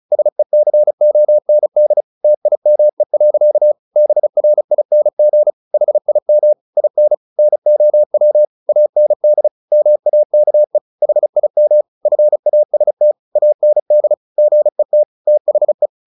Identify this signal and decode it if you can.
Morse code, heard as a keyed tone, about 34 wpm, 600 Hz. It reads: SECOND TIME. BRING HIM IN NOW AND MAKE HIM FAST AND GET THE
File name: the old man and the sea 2 35wpm